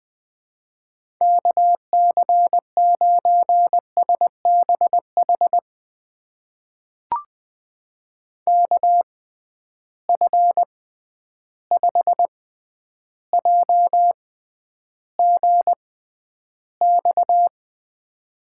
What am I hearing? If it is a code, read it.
Morse code, heard as a keyed tone, about 20 wpm, spaced as for 10 wpm, 700 Hz. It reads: KC9SBH KF5JGX
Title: USA Call Signs - 6 Characters - 6x Character Spacing - 20wpm